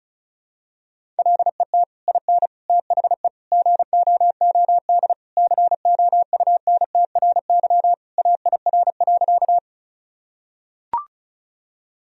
Morse code, heard as a keyed tone, about 34 wpm, 700 Hz. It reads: LET IN THE GOOD COUNTRY AIR.
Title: Sentences from Top 1000 Words - Review Part 1 - 35wpm